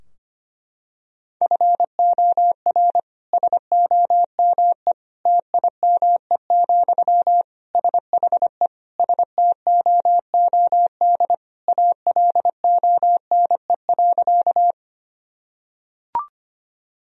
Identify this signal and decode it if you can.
Morse code, heard as a keyed tone, about 25 wpm, 700 Hz. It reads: FOR SOME TIME, SHE STOOD ALONE.
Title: Sentences from Top 1000 Words - Review Part 2 - 25wpm